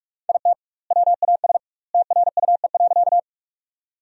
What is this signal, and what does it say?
Morse code, heard as a keyed tone, about 45 wpm, 700 Hz. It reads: IT WAS TRUE.
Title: 2x Word-Spacing - Sentences from Top 400 Words - 45wpm